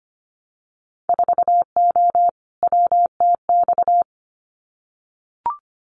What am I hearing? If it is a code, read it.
Morse code, heard as a keyed tone, about 25 wpm, 700 Hz. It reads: 4O WTX
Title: Winter Field Day Exchanges Encore 25wpm